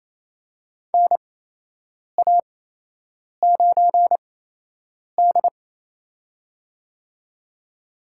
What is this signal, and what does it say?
Morse code, heard as a keyed tone, about 28 wpm, spaced as for 8 wpm, 700 Hz. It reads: NA9D
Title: USA Call Signs - 4 Characters - 8x Character Spacing - 28wpm